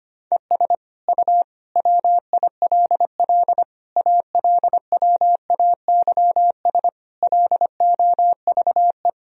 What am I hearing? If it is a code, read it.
Morse code, heard as a keyed tone, about 25 wpm, 700 Hz. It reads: ES U WILL ALWAYS LOVE
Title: Sentences from Top 400 Words and Abbreviations 25wpm